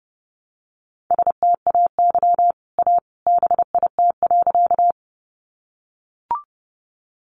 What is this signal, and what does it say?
Morse code, heard as a keyed tone, about 30 wpm, 700 Hz. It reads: STAY A BIT.
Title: Sentences from Top 900 Words 30wpm